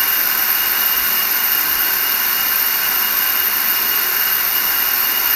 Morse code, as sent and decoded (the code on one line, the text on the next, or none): none